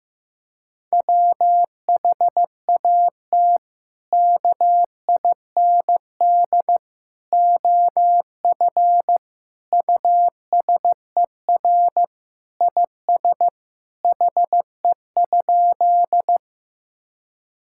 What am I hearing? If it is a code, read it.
Morse code, heard as a keyed tone, about 15 wpm, 700 Hz. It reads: WHAT KIND OF USER IS HE?
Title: Sentences from Top 1000 Words - Review Part 5 - 15wpm